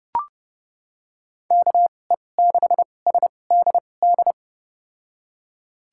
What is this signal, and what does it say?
Morse code, heard as a keyed tone, about 30 wpm, spaced as for 22 wpm, 700 Hz. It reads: KE6SDD
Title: USA Call Signs - 6 Characters - 2x Character Spacing - 30wpm